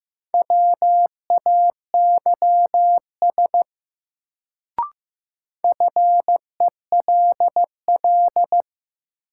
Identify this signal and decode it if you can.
Morse code, heard as a keyed tone, about 15 wpm, 700 Hz. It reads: WAYS FELL
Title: Sets of 5 Words Encore - New Word Drill - 15wpm